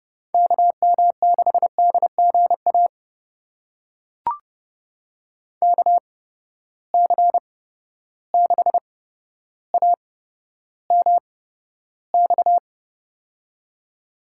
Morse code, heard as a keyed tone, about 31 wpm, spaced as for 13 wpm, 700 Hz. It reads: KM6DGA KC6AMX
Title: USA Call Signs - 6 Characters - 8x Character Spacing - 30wpm